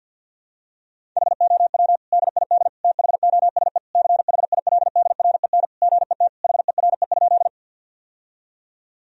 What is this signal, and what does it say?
Morse code, heard as a keyed tone, about 52 wpm, 700 Hz. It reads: HOW DID THOSE CHILDREN GET HERE?
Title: Sentences from Top 1000 Words - Review Part 1 - 50wpm